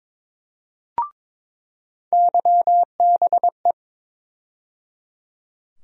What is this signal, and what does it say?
Morse code, heard as a keyed tone, about 22 wpm, 700 Hz. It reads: YBE
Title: USA Call Sign Postfixes - 3 Letters 22wpm